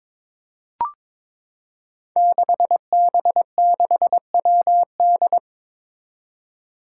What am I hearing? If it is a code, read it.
Morse code, heard as a keyed tone, about 22 wpm, 700 Hz. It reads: 6B6WD
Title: Five Challenging Letter-Number Combinations 22wpm